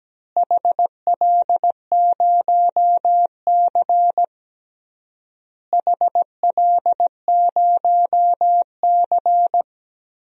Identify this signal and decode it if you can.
Morse code, heard as a keyed tone, about 17 wpm, 700 Hz. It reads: HL0C HL0C
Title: DXCC-100 Call Signs - 4 Characters - Repeated 3x - 17wpm